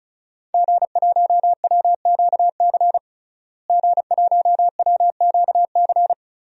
Morse code, heard as a keyed tone, about 36 wpm, 700 Hz. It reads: G1WQC G1WQC
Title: DXCC-100 Call Signs - 5 Characters - Repeated 2x - 35wpm